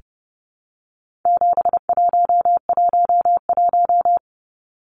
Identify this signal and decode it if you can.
Morse code, heard as a keyed tone, about 29 wpm, 700 Hz. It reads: 7111
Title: Numbers - 4-Digit 30wpm